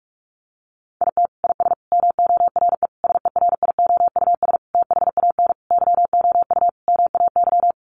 Morse code, heard as a keyed tone, about 44 wpm, 700 Hz. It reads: IT IS MORE SERIOUS THAN YOU MAY
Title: Sentences from Top 1100 Words 45wpm